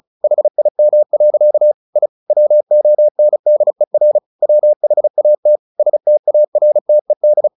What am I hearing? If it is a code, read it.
Morse code, heard as a keyed tone, about 34 wpm, 600 Hz. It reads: HIM. I WONDER WHAT STARTED
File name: the old man and the sea 2 35wpm